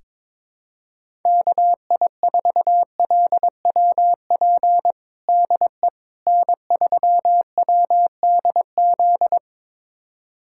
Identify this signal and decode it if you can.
Morse code, heard as a keyed tone, about 22 wpm, 700 Hz. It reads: KI4LWP DE N3WDZ
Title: USA Callsign DE Callsign 22wpm